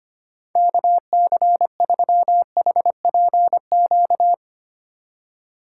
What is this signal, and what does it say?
Morse code, heard as a keyed tone, about 25 wpm, 700 Hz. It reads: KC3HPQ
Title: USA Call Signs - 4x Character Spacing - 25wpm